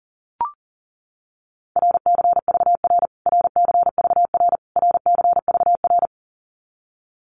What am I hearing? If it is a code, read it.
Morse code, heard as a keyed tone, about 40 wpm, 700 Hz. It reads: RCVR RCVR RCVR
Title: Abbreviations Repeated 3x - 40wpm